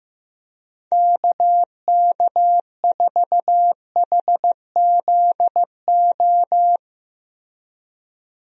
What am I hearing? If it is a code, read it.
Morse code, heard as a keyed tone, about 15 wpm, 700 Hz. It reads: KK4HZO
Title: USA Call Signs - 6 Characters - Repeated 2x - 15wpm